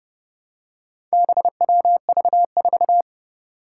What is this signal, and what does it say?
Morse code, heard as a keyed tone, about 31 wpm, 700 Hz. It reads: BWV4
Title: Four Challenging Letter-Number Combinations 30wpm